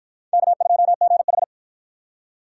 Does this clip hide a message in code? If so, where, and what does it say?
Yes: Morse code from 0.3 to 1.4 s, heard as a keyed tone, about 53 wpm, 700 Hz. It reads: K2GH